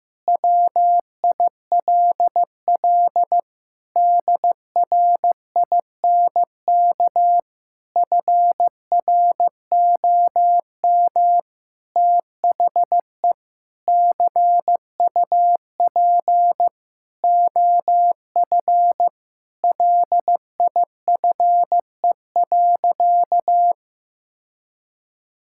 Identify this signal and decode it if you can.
Morse code, heard as a keyed tone, about 15 wpm, 700 Hz. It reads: WILL DRINK FROM THE CUP OF LIFE.
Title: Sentences from Top 1600 Words 15wpm